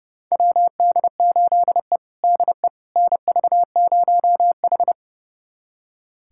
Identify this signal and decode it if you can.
Morse code, heard as a keyed tone, about 30 wpm, 700 Hz. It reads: WD8E DE NV0H
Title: USA Callsign DE Callsign - 4 Characters - 30wpm